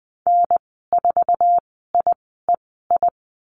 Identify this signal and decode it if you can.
Morse code, heard as a keyed tone, about 20 wpm, spaced as for 15 wpm, 700 Hz. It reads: N4IEI
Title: 082 Lesson 14g - Number 2 - Call Signs 20wpm